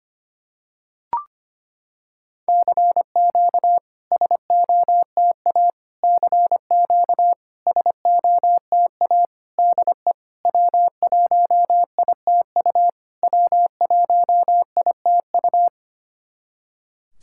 Morse code, heard as a keyed tone, about 25 wpm, 700 Hz. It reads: CQ SOTA CQ SOTA DE W1ITU W1ITU
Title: SOTA QSO Practice 25wpm